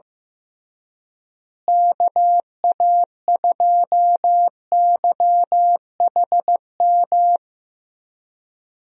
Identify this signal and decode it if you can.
Morse code, heard as a keyed tone, about 15 wpm, 700 Hz. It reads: KA2YHM
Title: USA Call Signs - 6 Characters - Repeated 3x - 15wpm